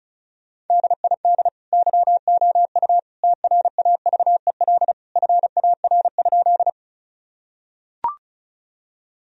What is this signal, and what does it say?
Morse code, heard as a keyed tone, about 34 wpm, 700 Hz. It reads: DID YOU TRAVEL FAR?
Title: Sentences from Top 1000 Words - Review Part 2 - 35wpm